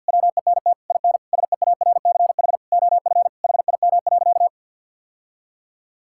Morse code, heard as a keyed tone, about 50 wpm, 700 Hz. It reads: WENT IN SEARCH OF HIM.
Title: Sentences from Top 1000 Words - Review Part 5 - 50wpm